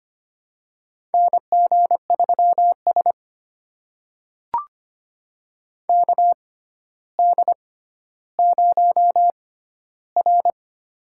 Morse code, heard as a keyed tone, about 25 wpm, spaced as for 13 wpm, 700 Hz. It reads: NG3S KD0R
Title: USA Call Signs - 6x Character Spacing - 25wpm